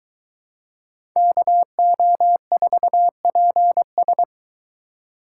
Morse code, heard as a keyed tone, about 23 wpm, 700 Hz. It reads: KO4PS